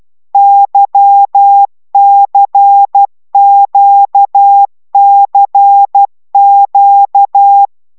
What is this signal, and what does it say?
Morse code, heard as a keyed tone, about 12 wpm, 800 Hz. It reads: YCQCQ
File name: taviro tanulas / cw 75 15 10